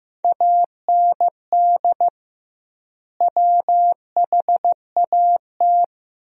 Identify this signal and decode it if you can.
Morse code, heard as a keyed tone, about 15 wpm, 700 Hz. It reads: AND WHAT
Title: 2x Word-Spacing - Sentences from Top 200 Words - 15wpm